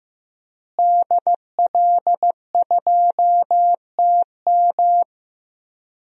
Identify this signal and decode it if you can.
Morse code, heard as a keyed tone, about 15 wpm, 700 Hz. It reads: DL2TM
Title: DXCC-100 Call Signs - 5 Characters - Repeated 3x - 15wpm